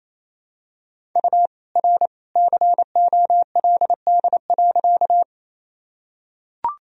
Morse code, heard as a keyed tone, about 28 wpm, 700 Hz. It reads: U R COLD.
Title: Sentences from Top 400 Words and Abbreviations 28wpm